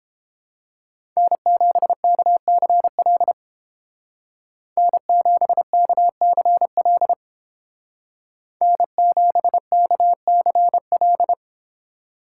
Morse code, heard as a keyed tone, about 29 wpm, 700 Hz. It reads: N7KCL N7KCL N7KCL